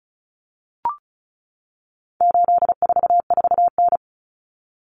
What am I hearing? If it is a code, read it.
Morse code, heard as a keyed tone, about 34 wpm, 700 Hz. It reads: 844N
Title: Common Cut Numbers - 4-Digit 35wpm